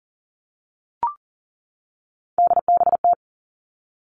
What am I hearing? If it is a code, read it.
Morse code, heard as a keyed tone, about 42 wpm, 700 Hz. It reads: DBT
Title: Three Challenging Letter-Number Combinations 40wpm